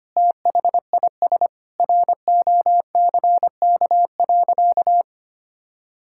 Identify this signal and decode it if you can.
Morse code, heard as a keyed tone, about 25 wpm, 700 Hz. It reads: THIS ROCK.
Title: Sentences from Top 1600 Words 25wpm